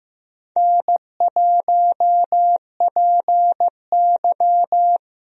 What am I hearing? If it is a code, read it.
Morse code, heard as a keyed tone, about 15 wpm, 700 Hz. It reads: N1PY
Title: USA Call Signs - 4 Characters - Repeated 2x - 15wpm